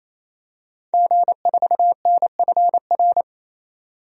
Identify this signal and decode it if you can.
Morse code, heard as a keyed tone, about 28 wpm, 700 Hz. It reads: G4NFR
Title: DXCC-100 Call Signs - 5 Characters - 28wpm